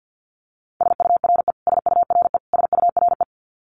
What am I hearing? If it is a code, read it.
Morse code, heard as a keyed tone, about 50 wpm, 700 Hz. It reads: SURE SURE SURE
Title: Top 500 Words Repeated 50wpm